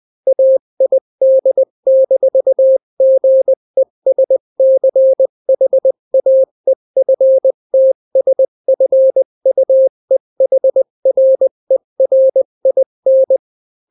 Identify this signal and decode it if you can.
Morse code, heard as a keyed tone, about 20 wpm, 530 Hz. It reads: AID-GESCHAEFTSFUEHRERIN
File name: daily / 18wpm